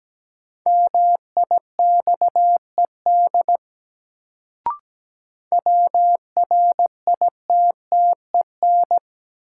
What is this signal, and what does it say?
Morse code, heard as a keyed tone, about 17 wpm, 700 Hz. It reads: MIXED WRITTEN
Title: Sentences from Top 1500 Words - New Word Drill Alt - 17wpm